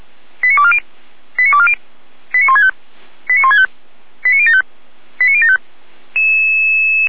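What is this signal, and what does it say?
Five-tone selective call (ZVEI), heard as quick runs of five tones, tones of about 70 ms. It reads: 89240 89240 89165 89165 89075 89075
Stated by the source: ZVEI five-tone fire brigade alert for code 89250, received on an SDR by Feuerwehr Orlamünde.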